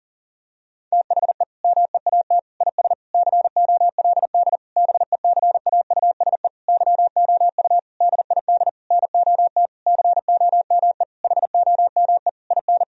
Morse code, heard as a keyed tone, about 40 wpm, 700 Hz. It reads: THE MEAT IS COLD BECAUSE YOU DID NOT COME HOME IN
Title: Sentences from Top 1000 Words - Review Part 3 - 40wpm